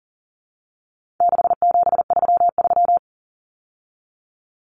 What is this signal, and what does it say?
Morse code, heard as a keyed tone, about 40 wpm, 700 Hz. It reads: 6733
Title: Numbers - 4-Digit 40wpm